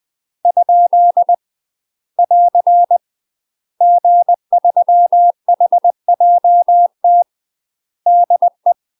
Morse code, heard as a keyed tone, about 20 wpm, 700 Hz. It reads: ? <AR> G3HJT DE
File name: UK MASTER QSOs TRAINING / 20WPM / UK MASTER QSOs 0061